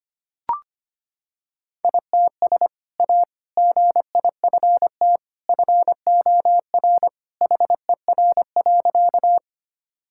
Morse code, heard as a keyed tone, about 25 wpm, 700 Hz. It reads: ITS A GIFT FOR HER.
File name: practice / 25wpm / Sentences with 2-Word Phrases - 25wpm